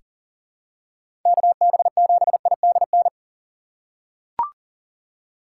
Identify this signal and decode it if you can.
Morse code, heard as a keyed tone, about 40 wpm, 700 Hz. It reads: KB7IDN
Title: USA Call Signs - 6 Characters - 4x Character Spacing - 40wpm